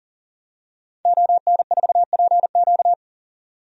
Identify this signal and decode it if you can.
Morse code, heard as a keyed tone, about 40 wpm, 700 Hz. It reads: ON4PQ
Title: DXCC-100 Call Signs - 5 Characters - 40wpm